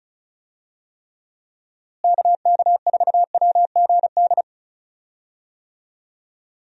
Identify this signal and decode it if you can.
Morse code, heard as a keyed tone, about 36 wpm, 700 Hz. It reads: KK4WGD
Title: USA Call Signs - 6 Characters - Repeated 3x - 35wpm